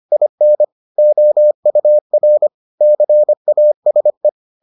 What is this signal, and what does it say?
Morse code, heard as a keyed tone, about 25 wpm, 600 Hz. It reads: IN OUR CASE